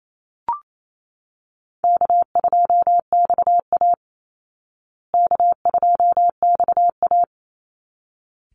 Morse code, heard as a keyed tone, about 28 wpm, 700 Hz. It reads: K2XA K2XA
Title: Easy Call Signs Repeated 2x - 28wpm